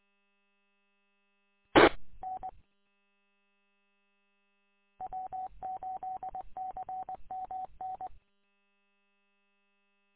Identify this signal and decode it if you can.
Morse code, heard as a keyed tone, about 20 wpm, 750 Hz. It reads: T N W8CMN
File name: w8cmn- -combined